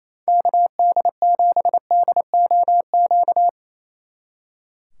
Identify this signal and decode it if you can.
Morse code, heard as a keyed tone, about 36 wpm, 700 Hz. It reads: KD7DOQ